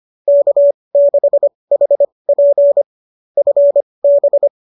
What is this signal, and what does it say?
Morse code, heard as a keyed tone, about 25 wpm, spaced as for 20 wpm, 590 Hz. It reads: K6HP FB
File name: qso205 20